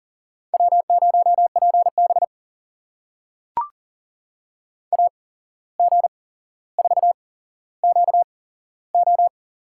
Morse code, heard as a keyed tone, about 40 wpm, spaced as for 16 wpm, 700 Hz. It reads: W0PB AG4QO
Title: USA Call Signs - 8x Character Spacing - 40wpm